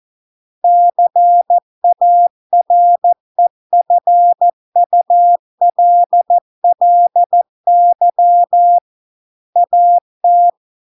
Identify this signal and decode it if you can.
Morse code, heard as a keyed tone, about 14 wpm, 700 Hz. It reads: CAREFULLY AT